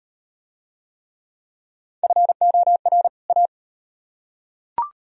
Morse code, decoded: FOR A

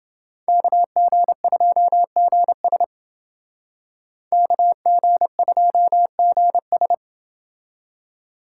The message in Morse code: KG2GS KG2GS

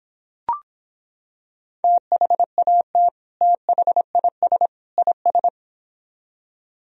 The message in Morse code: THAT THIS IS

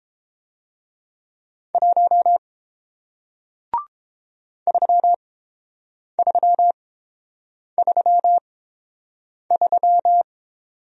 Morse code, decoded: 1 3333

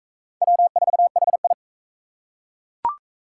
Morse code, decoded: W4HI

